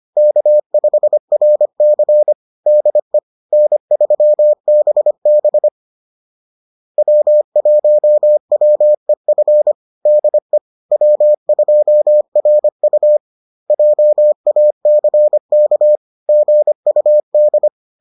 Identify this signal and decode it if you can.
Morse code, heard as a keyed tone, about 25 wpm, 600 Hz. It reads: K5RC DE N3BB W1WEF DE W2RU JACK GUD